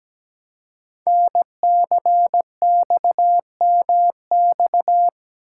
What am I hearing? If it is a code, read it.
Morse code, heard as a keyed tone, about 17 wpm, 700 Hz. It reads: NCXMX
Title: Code Groups - 5 Letters - 17wpm